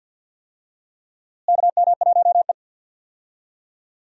Morse code, heard as a keyed tone, about 50 wpm, 700 Hz. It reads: KM1E
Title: USA Call Signs - 4 Characters - Repeated 2x - 50wpm